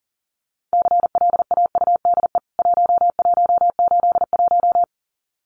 Morse code, heard as a keyed tone, about 37 wpm, 700 Hz. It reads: CLAUDE 1181
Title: CWT Exchange 40wpm